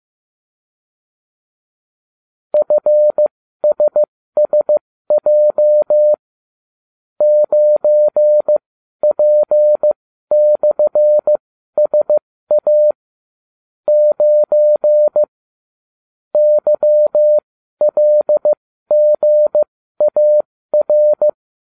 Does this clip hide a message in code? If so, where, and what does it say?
Yes: Morse code from 2.5 to 21.3 s, heard as a keyed tone, about 15 wpm, spaced as for 12 wpm, 600 Hz. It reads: FSSJ 9P/SA 9 YLGAR